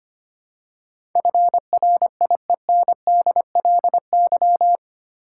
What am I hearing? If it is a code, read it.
Morse code, heard as a keyed tone, about 25 wpm, 700 Hz. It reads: FRIENDLY
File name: 8 Letter Words - First 3 Letters Given - 25wpm